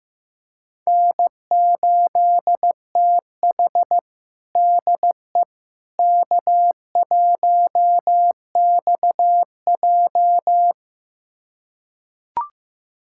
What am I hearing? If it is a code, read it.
Morse code, heard as a keyed tone, about 15 wpm, 700 Hz. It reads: N8TH DE K1XJ
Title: USA Callsign DE Callsign - 4 Characters - 15wpm